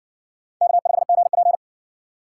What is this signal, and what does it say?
Morse code, heard as a keyed tone, about 56 wpm, 700 Hz. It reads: K5GP